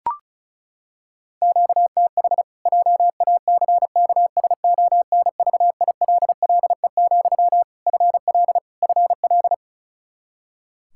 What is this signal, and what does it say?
Morse code, heard as a keyed tone, about 36 wpm, 700 Hz. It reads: QTH JACKSONVILLE, FL FL
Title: Practice QSO - Exchange 1 - Beginning - 35wpm